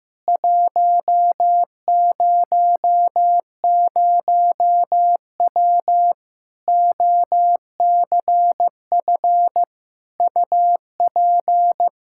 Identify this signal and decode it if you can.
Morse code, heard as a keyed tone, about 15 wpm, 700 Hz. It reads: 100W OCF UP
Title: K1USN SST QSO Practice 15wpm